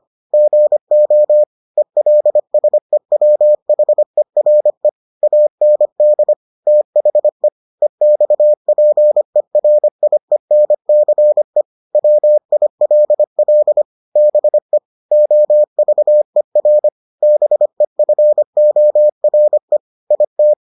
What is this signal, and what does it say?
Morse code, heard as a keyed tone, about 25 wpm, 600 Hz. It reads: GO ELSEWHERE AND THE EXPERIENCE WILL BE OVER BEFORE IT